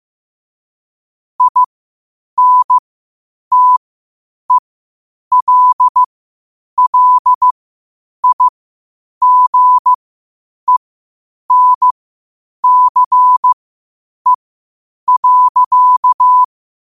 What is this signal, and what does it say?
Morse code, heard as a keyed tone, about 15 wpm, spaced as for 8 wpm, 1000 Hz. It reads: INTELLIGENCE.
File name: qotdfiles / 8wpm